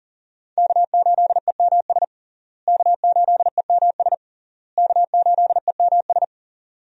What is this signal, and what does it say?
Morse code, heard as a keyed tone, about 37 wpm, 700 Hz. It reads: K8EMS K8EMS K8EMS
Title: USA Call Signs - 5 Characters - Repeated 3x - 40wpm